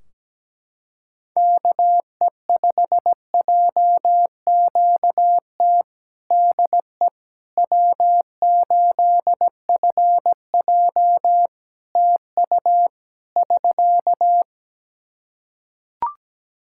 Morse code, heard as a keyed tone, about 17 wpm, 700 Hz. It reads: KE5JQT DE W8FJ TU <SK>
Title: Practice QSO - Exchange 4 - Ending - 17wpm